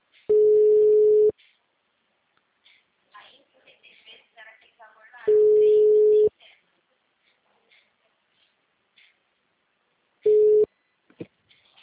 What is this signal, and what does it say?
Ringback tone (Europe, 425 Hz), with speech.